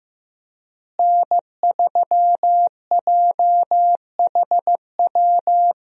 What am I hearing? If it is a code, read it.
Morse code, heard as a keyed tone, about 15 wpm, 700 Hz. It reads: N3JHW
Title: USA Call Signs - 5 Characters - Repeated 2x - 15wpm